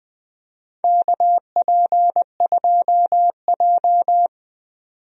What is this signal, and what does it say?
Morse code, heard as a keyed tone, about 20 wpm, 700 Hz. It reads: KP2J
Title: DXCC-100 Call Signs - 4 Characters - 20wpm